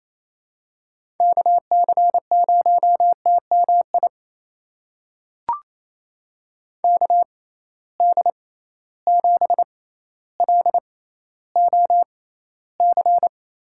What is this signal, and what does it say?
Morse code, heard as a keyed tone, about 28 wpm, spaced as for 14 wpm, 700 Hz. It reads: KC0TMI KD7LOC
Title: USA Call Signs - 6 Characters - 6x Character Spacing - 28wpm